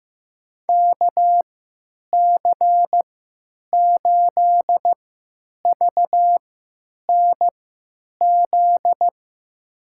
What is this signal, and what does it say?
Morse code, heard as a keyed tone, about 15 wpm, spaced as for 9 wpm, 700 Hz. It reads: KC8VNZ
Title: USA Call Signs - 6 Characters - 3x Character Spacing - 15wpm